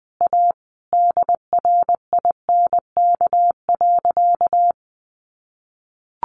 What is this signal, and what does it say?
Morse code, heard as a keyed tone, about 20 wpm, 700 Hz. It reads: A DRINK.
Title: Sentences from Top 1500 Words 20wpm